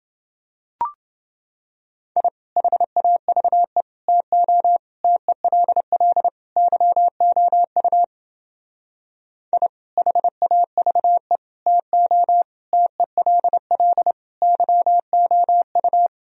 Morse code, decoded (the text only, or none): I HAVE TO TELL YOU I HAVE TO TELL YOU